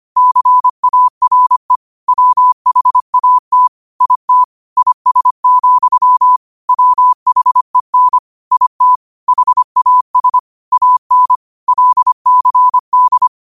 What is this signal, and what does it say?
Morse code, heard as a keyed tone, about 25 wpm, 1000 Hz. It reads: CARE WHAT IT IS, WHEN IT HAS AN LCD
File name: qotdfiles / 25wpm